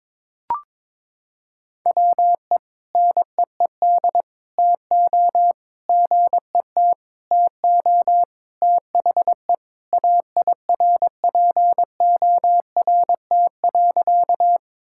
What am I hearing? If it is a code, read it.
Morse code, heard as a keyed tone, about 21 wpm, 700 Hz. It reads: WE NEED TO GET TO THE AIRPORT.
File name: practice / 22wpm / Sentences with 3-Word Phrases - 22wpm